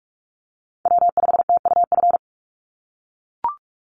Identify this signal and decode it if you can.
Morse code, heard as a keyed tone, about 44 wpm, 700 Hz. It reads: W5TUF
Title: Call Signs - 45wpm